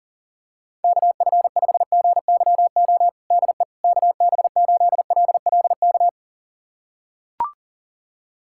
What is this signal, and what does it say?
Morse code, heard as a keyed tone, about 40 wpm, 700 Hz. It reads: KF5GYO DE KB8LLK
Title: USA Callsign DE Callsign 40wpm